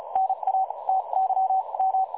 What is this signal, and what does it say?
Morse code, heard as a keyed tone, about 36 wpm, spaced as for 29 wpm, 760 Hz. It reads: NA TYM
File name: dlazaawansowanych / nagraniazpasma / websdr recording 3565.0kHz